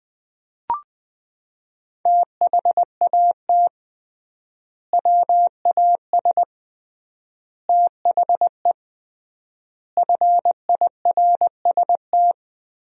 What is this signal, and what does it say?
Morse code, heard as a keyed tone, about 20 wpm, 700 Hz. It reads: THAT WAS THE FIRST
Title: 3x Word-Spacing - Sentences from Top 300 Words - 20wpm